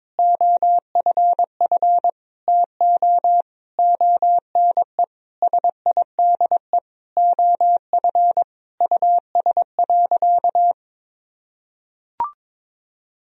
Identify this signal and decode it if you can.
Morse code, heard as a keyed tone, about 22 wpm, 700 Hz. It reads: OFF TO ONE SIDE OF US.
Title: Sentences from Top 1000 Words - Review Part 3 - 22wpm